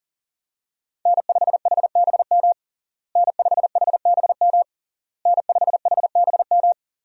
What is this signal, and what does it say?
Morse code, heard as a keyed tone, about 37 wpm, 700 Hz. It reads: N5HBM N5HBM N5HBM
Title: USA Call Signs - 5 Characters - Repeated 3x - 40wpm